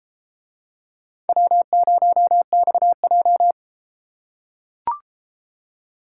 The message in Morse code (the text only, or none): W0XJ